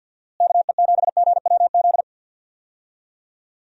Morse code, heard as a keyed tone, about 47 wpm, 700 Hz. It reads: KE7GWZ
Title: USA Call Signs - 6 Characters - Repeated 3x - 50wpm